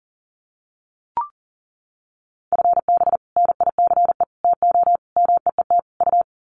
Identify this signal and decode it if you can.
Morse code, heard as a keyed tone, about 40 wpm, 700 Hz. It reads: FB NICE TO MEET U
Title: Common QSO Phrases 40wpm